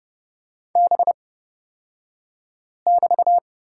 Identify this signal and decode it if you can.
Morse code, heard as a keyed tone, about 29 wpm, spaced as for 5 wpm, 690 Hz. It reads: B<BT>
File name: Lesson 36d - BT - Two Characters 30wpm